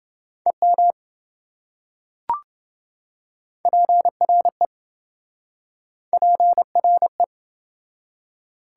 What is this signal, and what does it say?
Morse code, heard as a keyed tone, about 26 wpm, 700 Hz. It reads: EM PRE PRE